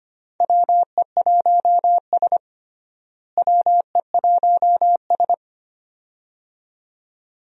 Morse code, heard as a keyed tone, about 25 wpm, 700 Hz. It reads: WE1S WE1S
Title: USA Call Signs - 4 Characters - Repeated 2x - 25wpm